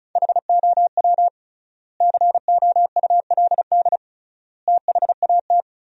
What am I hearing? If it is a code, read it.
Morse code, heard as a keyed tone, about 34 wpm, 700 Hz. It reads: HOW COULD THAT
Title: 3x Word-Spacing - Sentences from Top 400 Words - 35wpm